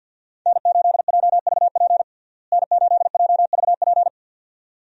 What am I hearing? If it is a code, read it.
Morse code, heard as a keyed tone, about 53 wpm, 700 Hz. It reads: N8JVP N8JVP
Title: USA Call Signs - 5 Characters - Repeated 2x - 50wpm